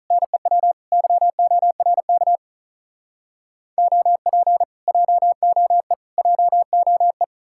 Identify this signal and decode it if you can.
Morse code, heard as a keyed tone, about 48 wpm, 700 Hz. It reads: NEW YORK OP JOE JOE